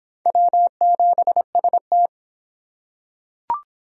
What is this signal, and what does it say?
Morse code, heard as a keyed tone, about 25 wpm, 700 Hz. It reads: W7ST